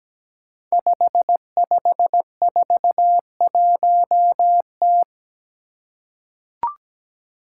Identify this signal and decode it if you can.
Morse code, heard as a keyed tone, about 17 wpm, 700 Hz. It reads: 5541T
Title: Five Challenging Letter-Number Combinations 17wpm